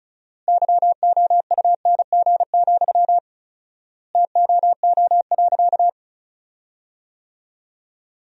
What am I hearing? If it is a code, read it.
Morse code, heard as a keyed tone, about 36 wpm, 700 Hz. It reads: YOUNG, TOO.
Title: 4x Word-Spacing - Sentences from Top 400 Words - 35wpm